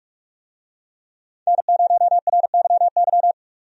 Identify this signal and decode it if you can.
Morse code, heard as a keyed tone, about 45 wpm, 700 Hz. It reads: N0RYY
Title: USA Call Signs - 5 Characters - 45wpm